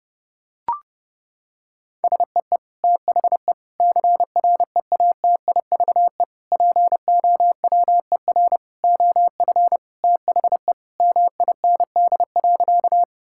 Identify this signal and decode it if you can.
Morse code, heard as a keyed tone, about 30 wpm, 700 Hz. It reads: SEE THE CREATIVE POWER OF THE MIND.
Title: Sentences from Top 1000 Words - Review Part 5 - 30wpm